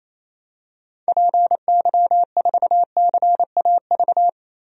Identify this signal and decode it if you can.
Morse code, heard as a keyed tone, about 28 wpm, 700 Hz. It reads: PY4CAV
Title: DXCC-100 Call Signs - 6 Characters - 28wpm